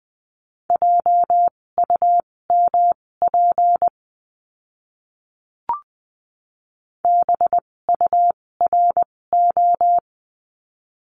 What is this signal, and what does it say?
Morse code, heard as a keyed tone, about 20 wpm, spaced as for 16 wpm, 700 Hz. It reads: JUMP BURO
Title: 256 Lesson 42a - Words 12-20wpm